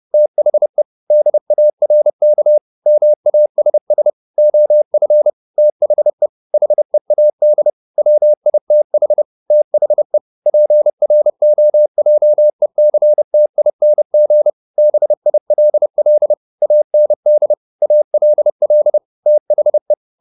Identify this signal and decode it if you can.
Morse code, heard as a keyed tone, about 30 wpm, 600 Hz. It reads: THE DARK MASS OF THE HEAD WITH THE PROJECTING BILL AND ALL THE